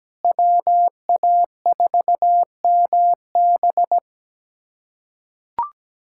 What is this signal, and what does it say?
Morse code, heard as a keyed tone, about 17 wpm, 700 Hz. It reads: WA4MB
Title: USA Callsign DE Callsign - 5 Characters - 17wpm